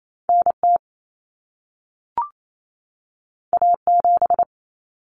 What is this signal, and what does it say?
Morse code, heard as a keyed tone, about 28 wpm, 700 Hz. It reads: NT A7
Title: Common Cut Numbers - 2-Digit 28wpm